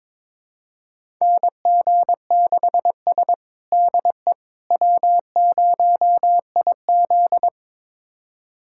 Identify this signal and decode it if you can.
Morse code, heard as a keyed tone, about 22 wpm, 700 Hz. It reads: NG6S DE W0IZ
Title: USA Callsign DE Callsign - 4 Characters - 22wpm